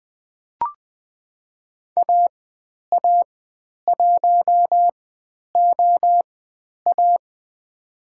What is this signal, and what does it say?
Morse code, heard as a keyed tone, about 20 wpm, spaced as for 10 wpm, 700 Hz. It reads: AA1OA
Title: 095 Lesson 16g - Letter c - Call Signs 20wpm